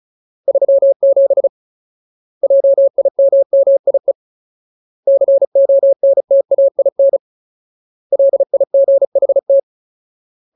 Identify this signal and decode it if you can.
Morse code, heard as a keyed tone, about 34 wpm, 550 Hz. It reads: 37 JIMMIE CONTAIN LIGHT
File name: Ultimate Word List / 35WPM Ultimate